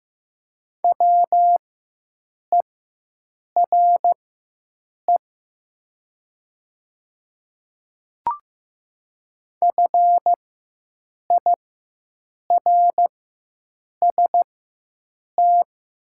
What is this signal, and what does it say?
Morse code, heard as a keyed tone, about 15 wpm, spaced as for 7 wpm, 700 Hz. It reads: WERE FIRST
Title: 4x Character Spacing - Top 500 Words - 15wpm